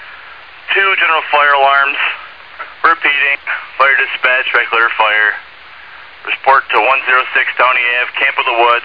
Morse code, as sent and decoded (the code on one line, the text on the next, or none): none